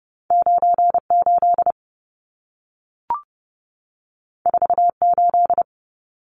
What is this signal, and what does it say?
Morse code, heard as a keyed tone, about 30 wpm, 700 Hz. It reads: 98 48
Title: Numbers - 2-Digit 30wpm